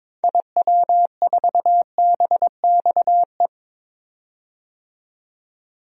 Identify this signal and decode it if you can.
Morse code, heard as a keyed tone, about 21 wpm, 700 Hz. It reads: IW4BXE
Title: DXCC-100 Call Signs - 6 Characters - 22wpm